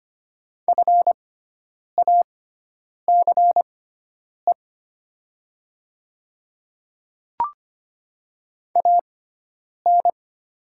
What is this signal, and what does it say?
Morse code, heard as a keyed tone, about 25 wpm, spaced as for 9 wpm, 700 Hz. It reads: FACE AN
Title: 6x Character Spacing - Top 500 Words - 25wpm